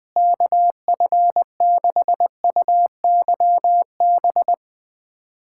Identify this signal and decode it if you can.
Morse code, heard as a keyed tone, about 20 wpm, 700 Hz. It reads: KF6UYB
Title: USA Callsign DE Callsign - 6 Characters - 20wpm